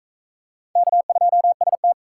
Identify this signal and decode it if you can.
Morse code, heard as a keyed tone, about 39 wpm, 700 Hz. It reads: K2ST